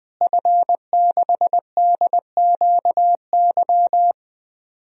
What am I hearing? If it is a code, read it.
Morse code, heard as a keyed tone, about 20 wpm, 700 Hz. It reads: F6DQY
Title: DXCC-100 Call Signs - 5 Characters - Repeated 3x - 20wpm